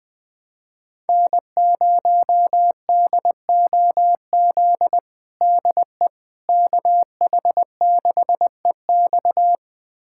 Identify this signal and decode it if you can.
Morse code, heard as a keyed tone, about 20 wpm, 700 Hz. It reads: N0DOZ DE KH6EX
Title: USA Callsign DE Callsign - 5 Characters - 20wpm